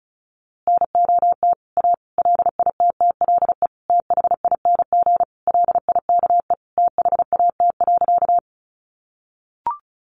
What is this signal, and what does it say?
Morse code, heard as a keyed tone, about 34 wpm, 700 Hz. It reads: NOT A LITTLE THING LIKE THAT.
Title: A Sound Of Thunder 35wpm